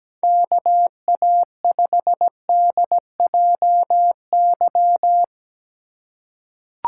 Morse code, heard as a keyed tone, about 17 wpm, 700 Hz. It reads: KA5DJY
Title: USA Call Signs - 6 Characters - 4x Character Spacing - 17wpm